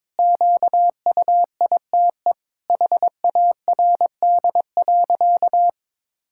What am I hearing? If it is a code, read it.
Morse code, heard as a keyed tone, about 22 wpm, 700 Hz. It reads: QUITE HARD.
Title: Pangram Challenge v2 - 22wpm